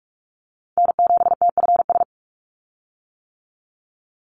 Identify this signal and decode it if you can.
Morse code, heard as a keyed tone, about 44 wpm, 700 Hz. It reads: N7TFS
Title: USA Call Signs 45wpm